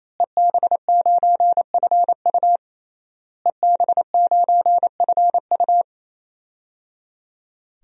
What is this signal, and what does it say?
Morse code, heard as a keyed tone, about 28 wpm, 700 Hz. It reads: EB9FU EB9FU
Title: DXCC-100 Call Signs - 5 Characters - Repeated 3x - 28wpm